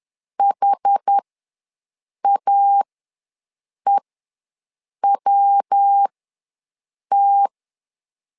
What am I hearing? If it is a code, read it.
Morse code, heard as a keyed tone, about 10 wpm, spaced as for 6 wpm, 800 Hz. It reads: HAEWT
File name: LEZIONI / 3 MWRO / 3 MW ws 10wpm 2213